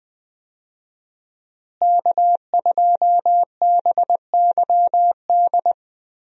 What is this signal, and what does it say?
Morse code, heard as a keyed tone, about 20 wpm, 700 Hz. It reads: K2BYD
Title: USA Call Signs - 5 Characters - Repeated 2x - 20wpm